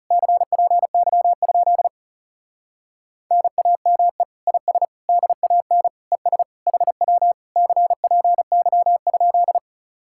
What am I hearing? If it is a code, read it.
Morse code, heard as a keyed tone, about 40 wpm, 700 Hz. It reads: CPY? NAME IS DAN ES HW CPY?